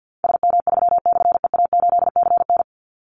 Morse code, heard as a keyed tone, about 50 wpm, 700 Hz. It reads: SM3/EA8CN
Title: Call Signs - 50wpm